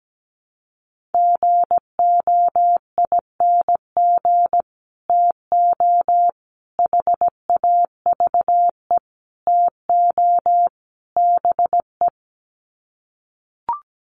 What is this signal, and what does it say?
Morse code, heard as a keyed tone, about 17 wpm, 700 Hz. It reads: GOING TO HAVE TO BE
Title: Sets of 5 Words 17wpm